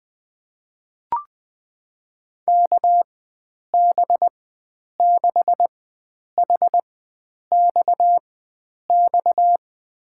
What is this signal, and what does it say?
Morse code, heard as a keyed tone, about 21 wpm, spaced as for 9 wpm, 700 Hz. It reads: KB6HXX
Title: USA Call Signs - 6 Characters - 4x Character Spacing - 20wpm